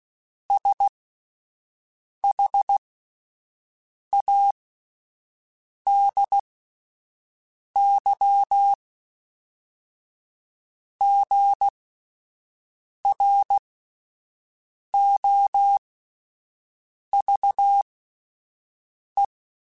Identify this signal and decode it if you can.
Morse code, heard as a keyed tone, about 16 wpm, spaced as for 5 wpm, 800 Hz. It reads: SHADY GROVE